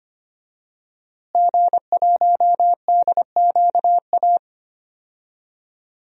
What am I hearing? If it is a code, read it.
Morse code, heard as a keyed tone, about 25 wpm, 700 Hz. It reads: G1DQA
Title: DXCC-100 Call Signs - 5 Characters - Repeated 2x - 25wpm